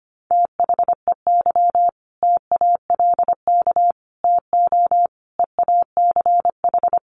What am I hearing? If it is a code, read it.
Morse code, heard as a keyed tone, about 25 wpm, 700 Hz. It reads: THEY TALK TO EACH
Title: Sentences from Top 1400 Words 25wpm